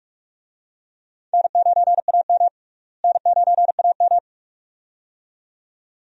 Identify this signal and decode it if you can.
Morse code, heard as a keyed tone, about 46 wpm, 700 Hz. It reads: N9AM N9AM
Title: USA Call Signs - 4 Characters - Repeated 2x - 45wpm